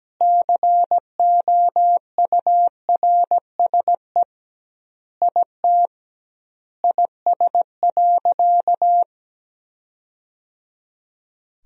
Morse code, heard as a keyed tone, about 17 wpm, 700 Hz. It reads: COURSE IT IS.
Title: 2x Word-Spacing - Sentences from Top 500 Words - 17wpm